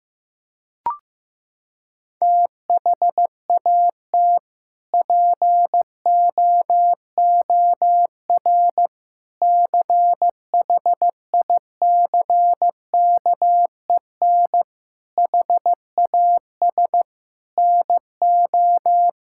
Morse code, heard as a keyed tone, about 15 wpm, 700 Hz. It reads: THAT POOR CHICKEN HAS NO
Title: Sentences from Top 1000 Words - Review Part 5 - 15wpm